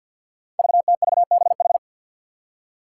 Morse code, heard as a keyed tone, about 50 wpm, 700 Hz. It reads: VTVBH
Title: Five Challenging Letter-Number Combinations 50wpm